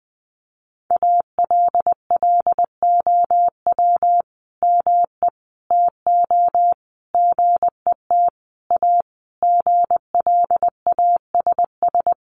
Morse code, heard as a keyed tone, about 20 wpm, 700 Hz. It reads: ALLOW ME TO GET A GLASS
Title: Sentences from Top 1300 Words 20wpm